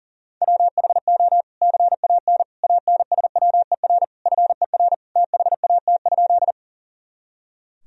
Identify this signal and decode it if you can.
Morse code, heard as a keyed tone, about 40 wpm, 700 Hz. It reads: WHO CAN ANSWER FER THAT?
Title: Sentences from Top 300 Words and Abbreviations 40wpm